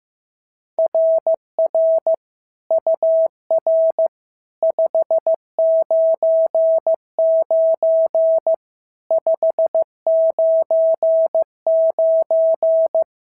Morse code, heard as a keyed tone, about 14 wpm, 650 Hz. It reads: RR UR 599 599
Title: SOTA QSO Practice 15wpm